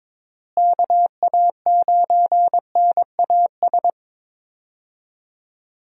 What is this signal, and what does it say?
Morse code, heard as a keyed tone, about 22 wpm, 700 Hz. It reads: KA9NAS
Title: USA Call Signs - 6 Characters - Repeated 2x - 22wpm